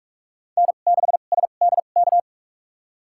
Morse code, heard as a keyed tone, about 42 wpm, spaced as for 31 wpm, 700 Hz. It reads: N6SDK
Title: USA Call Signs - 5 Characters - 2x Character Spacing - 45wpm